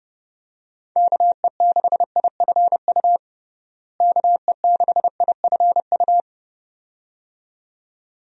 Morse code, heard as a keyed tone, about 30 wpm, 700 Hz. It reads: KE6IFU KE6IFU
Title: USA Call Signs - 6 Characters - Repeated 2x - 30wpm